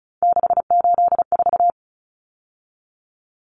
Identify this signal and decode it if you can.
Morse code, heard as a keyed tone, about 34 wpm, 700 Hz. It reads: 684
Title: Numbers - 3-Digit 35wpm